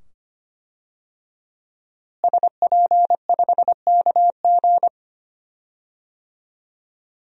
Morse code, heard as a keyed tone, about 25 wpm, 700 Hz. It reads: SP5KG
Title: DXCC-100 Call Signs - 5 Characters - Repeated 2x - 25wpm